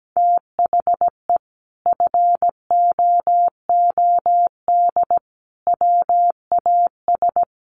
Morse code, heard as a keyed tone, about 17 wpm, 700 Hz. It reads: THE FOOD WAS
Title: Sentences from Top 1400 Words 17wpm